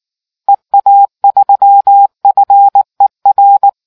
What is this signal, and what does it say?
Morse code, heard as a keyed tone, about 19 wpm, 800 Hz. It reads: EA3FER